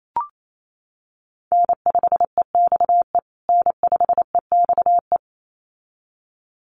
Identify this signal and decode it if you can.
Morse code, heard as a keyed tone, about 28 wpm, 700 Hz. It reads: N5EXE N5EXE
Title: USA Call Signs Repeated 2x 28wpm